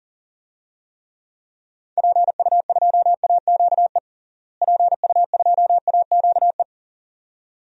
Morse code, heard as a keyed tone, about 40 wpm, 700 Hz. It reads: PU2AQE PU2AQE
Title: DXCC-100 Call Signs - 6 Characters - Repeated 2x - 40wpm